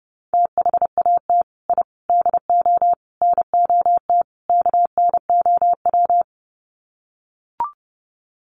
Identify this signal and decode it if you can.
Morse code, heard as a keyed tone, about 30 wpm, 700 Hz. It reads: THAT I DO NOT KNOW
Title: Sets of 5 Words Encore2 30wpm